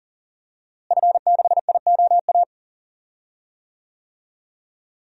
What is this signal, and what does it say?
Morse code, heard as a keyed tone, about 39 wpm, 700 Hz. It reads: F6IOA
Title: DXCC-100 Call Signs - 5 Characters - Repeated 3x - 40wpm